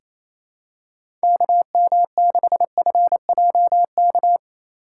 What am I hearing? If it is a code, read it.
Morse code, heard as a keyed tone, about 28 wpm, 700 Hz. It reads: KM6FJK
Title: USA Call Signs - 6 Characters - 2x Character Spacing - 28wpm